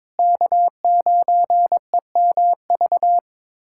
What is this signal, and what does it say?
Morse code, heard as a keyed tone, about 22 wpm, 700 Hz. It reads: K9EMV
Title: USA Callsign DE Callsign 22wpm